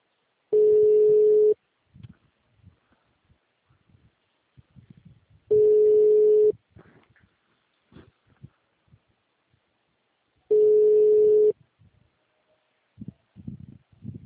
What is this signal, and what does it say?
Ringback tone (Europe, 425 Hz).
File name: out- -2035- - - .34